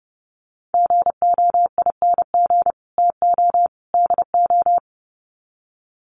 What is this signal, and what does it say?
Morse code, heard as a keyed tone, about 30 wpm, 700 Hz. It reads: GOING TO DO
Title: Sets of 3 Words 30wpm